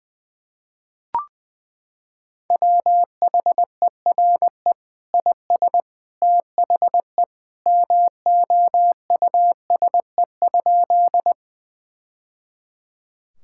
Morse code, heard as a keyed tone, about 21 wpm, 700 Hz. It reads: WHERE IS THE MOUSE?
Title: Sentences from Top 1000 Words - Review Part 5 - 20wpm